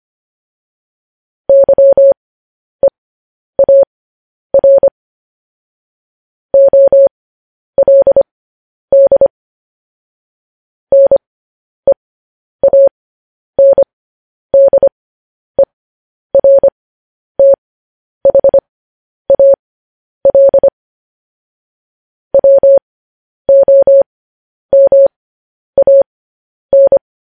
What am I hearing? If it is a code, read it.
Morse code, heard as a keyed tone, about 25 wpm, spaced as for 10 wpm, 550 Hz. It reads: YEAR OLD NEANDERTHAL WOMAN